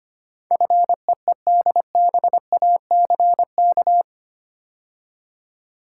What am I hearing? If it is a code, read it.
Morse code, heard as a keyed tone, about 25 wpm, 700 Hz. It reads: FEEDBACK
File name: 8 Letter Words - First 3 Letters Given - 25wpm